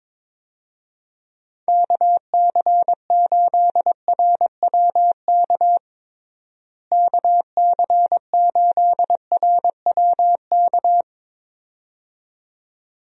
Morse code, heard as a keyed tone, about 21 wpm, 700 Hz. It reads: KC8RWK KC8RWK
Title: USA Call Signs - 6 Characters - Repeated 2x - 22wpm